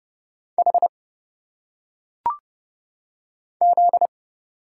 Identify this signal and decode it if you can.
Morse code, heard as a keyed tone, about 30 wpm, 700 Hz. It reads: H Z
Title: Single Letters - Phonetic Pronunciation 30wpm